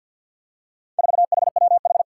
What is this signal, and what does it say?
Morse code, heard as a keyed tone, about 46 wpm, 700 Hz. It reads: 4HWH
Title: Four Challenging Letter-Number Combinations 50wpm